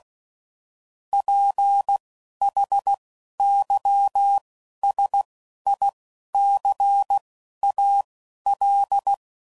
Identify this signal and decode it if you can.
Morse code, heard as a keyed tone, about 16 wpm, spaced as for 11 wpm, 800 Hz. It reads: PHYSICAL
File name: CW Code Groups @10WPM / Technical Subjects / SwPwrSup